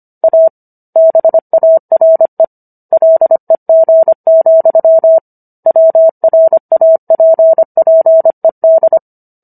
Morse code, heard as a keyed tone, about 25 wpm, 650 Hz. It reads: A BARE LEG, WRAPPED